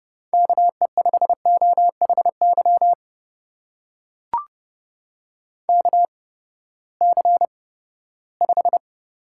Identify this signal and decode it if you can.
Morse code, heard as a keyed tone, about 31 wpm, spaced as for 17 wpm, 700 Hz. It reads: KE5OHY KC5
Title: USA Call Signs - 6 Characters - 8x Character Spacing - 30wpm